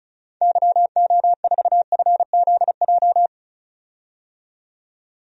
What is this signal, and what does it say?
Morse code, heard as a keyed tone, about 34 wpm, 700 Hz. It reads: YO4FZJ
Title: DXCC-100 Call Signs - 6 Characters - 35wpm